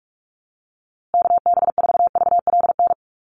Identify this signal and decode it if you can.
Morse code, heard as a keyed tone, about 44 wpm, 700 Hz. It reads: KB4VLN
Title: USA Call Signs Repeated 2x 45wpm